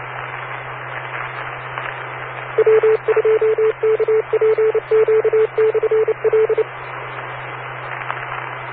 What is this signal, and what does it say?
Morse code, heard as a keyed tone, about 29 wpm, 440 Hz, with noise. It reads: W2KPQ/L